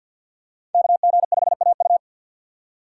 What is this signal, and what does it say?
Morse code, heard as a keyed tone, about 52 wpm, 700 Hz. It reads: KG5AU
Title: USA Call Signs - 5 Characters - 2x Character Spacing - 50wpm